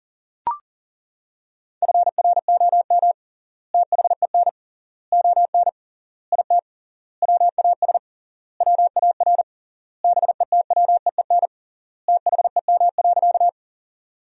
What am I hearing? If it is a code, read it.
Morse code, heard as a keyed tone, about 40 wpm, 700 Hz. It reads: FROM THEN ON IT WAS WAR BETWEEN THEM.
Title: 3x Word-Spacing - Sentences from Top 400 Words - 40wpm